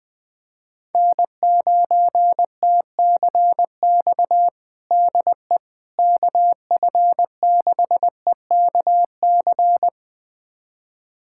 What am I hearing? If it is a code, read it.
Morse code, heard as a keyed tone, about 21 wpm, 700 Hz. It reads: N9TCX DE KF6EKC
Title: USA Callsign DE Callsign 20wpm